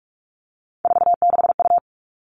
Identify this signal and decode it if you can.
Morse code, heard as a keyed tone, about 44 wpm, 700 Hz. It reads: 46U
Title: All Cut Numbers - 3-Digit 45wpm